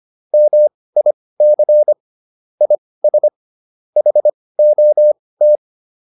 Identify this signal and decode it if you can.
Morse code, heard as a keyed tone, about 25 wpm, spaced as for 18 wpm, 600 Hz. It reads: MIC IS HOT